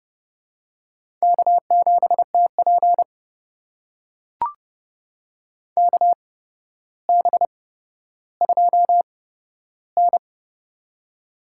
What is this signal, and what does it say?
Morse code, heard as a keyed tone, about 32 wpm, spaced as for 13 wpm, 700 Hz. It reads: K7TP KB2N
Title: USA Call Signs - 4 Characters - 8x Character Spacing - 30wpm